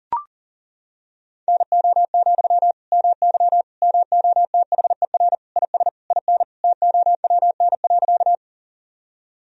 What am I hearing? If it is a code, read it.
Morse code, heard as a keyed tone, about 40 wpm, 700 Hz. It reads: NO, MY MOTHER IS IN TOWN.
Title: Sentences from Top 1000 Words - Review Part 2 - 40wpm